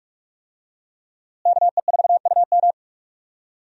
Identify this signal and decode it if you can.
Morse code, heard as a keyed tone, about 42 wpm, 700 Hz. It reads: KE4UM
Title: USA Call Signs - 5 Characters - Repeated 2x - 45wpm